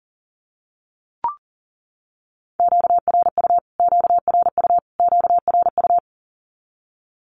Morse code, heard as a keyed tone, about 40 wpm, 700 Hz. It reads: QRU QRU QRU
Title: Q-Codes Repeated 3x 40wpm